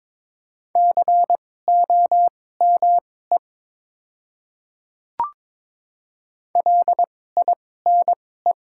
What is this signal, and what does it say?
Morse code, heard as a keyed tone, about 22 wpm, spaced as for 16 wpm, 700 Hz. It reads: COME LINE
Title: 2x Character Spacing - Top 200 Words - 22wpm